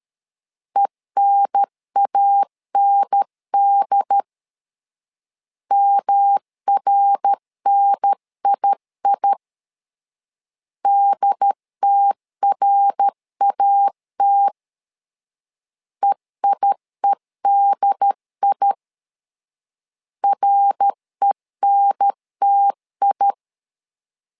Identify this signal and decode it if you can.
Morse code, heard as a keyed tone, about 13 wpm, 800 Hz. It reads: ENAND MRNII DTRAT EIEDI RENTI
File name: LEZIONI / 3 MWRO / 3 MWR 12wpm 1711